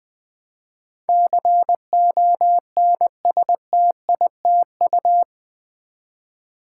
Morse code, heard as a keyed tone, about 20 wpm, 700 Hz. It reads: CONSTITU
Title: Build A Word - 4 Letters At A Time - 12-Letter Words - 20wpm